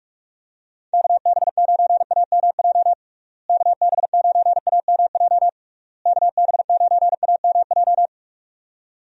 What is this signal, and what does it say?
Morse code, heard as a keyed tone, about 46 wpm, 700 Hz. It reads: KB9AMJ KB9AMJ KB9AMJ
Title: USA Call Signs - 6 Characters - Repeated 3x - 45wpm